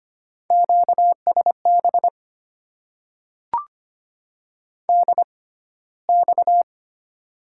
Morse code, heard as a keyed tone, about 25 wpm, spaced as for 15 wpm, 700 Hz. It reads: QSB DX
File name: Top 90 QSO Elements - 6x Character Spacing - 25wpm